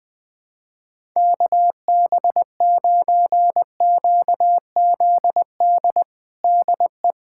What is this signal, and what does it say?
Morse code, heard as a keyed tone, about 21 wpm, 700 Hz. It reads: KB9QZD DE
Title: USA Callsign DE Callsign 20wpm